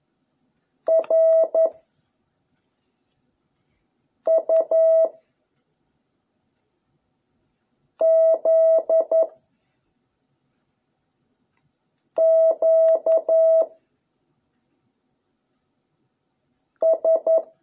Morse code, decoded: R U Z Q S

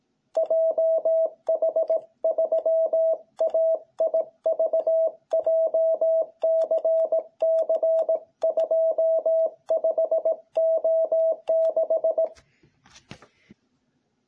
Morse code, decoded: JH3AIV JCC25O6

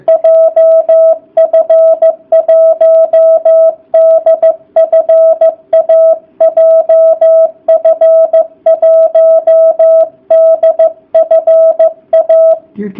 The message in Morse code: JF1DFA JF1DFA

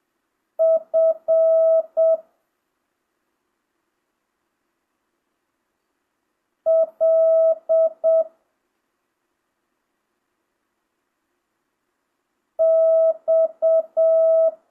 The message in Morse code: F L X

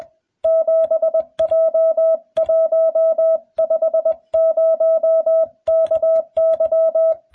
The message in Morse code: 7J150KY

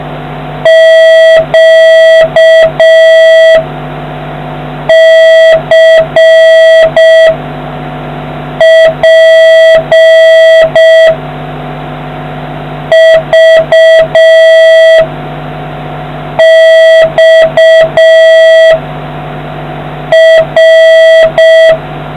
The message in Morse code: Q C P V X R

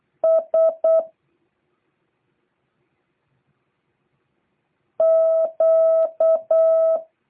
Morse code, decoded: S Q

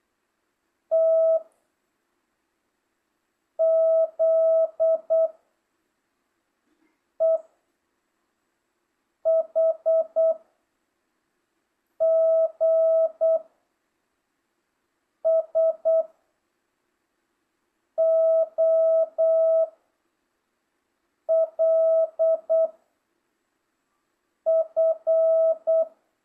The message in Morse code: T Z E H G S O L F

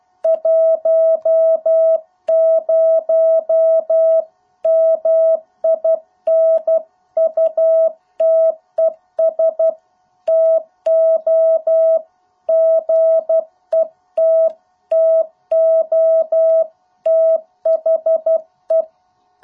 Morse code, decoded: 10 MINUTES TO GET TO THE

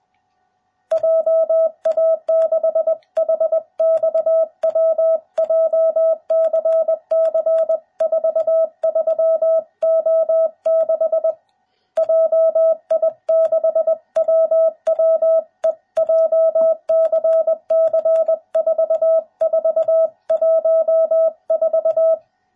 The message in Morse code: JA6HXW JCC 43O6 JI6WWE JCC 4414

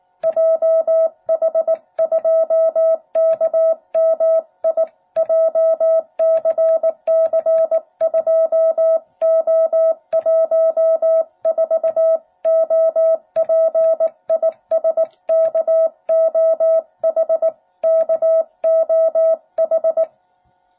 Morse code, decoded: JH2KMI JCC2O14 OP IS KOH KOH

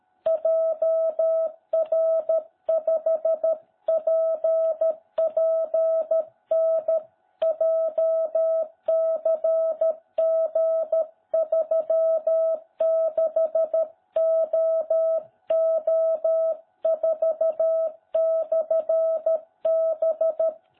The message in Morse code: JR5PPN JCG36OO4/B

